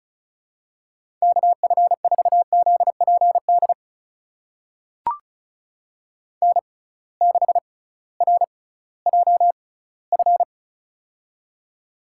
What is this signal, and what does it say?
Morse code, heard as a keyed tone, about 36 wpm, spaced as for 19 wpm, 700 Hz. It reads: KF4ZPD N6RJF
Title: USA Call Signs - 6x Character Spacing - 35wpm